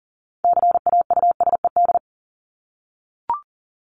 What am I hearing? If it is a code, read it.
Morse code, heard as a keyed tone, about 40 wpm, 700 Hz. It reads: CAUSED
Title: ED Suffix - Top 100 Words with Base Words 40wpm